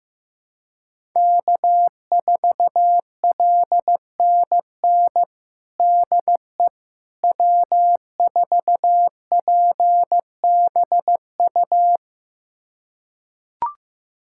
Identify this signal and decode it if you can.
Morse code, heard as a keyed tone, about 15 wpm, 700 Hz. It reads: K4LNN DE W4PBU
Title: USA Callsign DE Callsign - 5 Characters - 15wpm